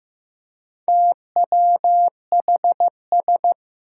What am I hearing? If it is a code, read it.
Morse code, heard as a keyed tone, about 15 wpm, 700 Hz. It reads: TWHS
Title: Four Challenging Letter-Number Combinations 15wpm